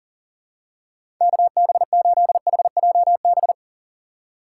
Morse code, heard as a keyed tone, about 40 wpm, 700 Hz. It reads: KB8HJB
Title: USA Call Signs - 6 Characters - 4x Character Spacing - 40wpm